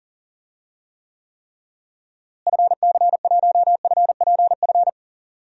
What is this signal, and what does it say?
Morse code, heard as a keyed tone, about 40 wpm, 700 Hz. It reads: FC1FPF
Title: DXCC-100 Call Signs - 6 Characters - Repeated 3x - 40wpm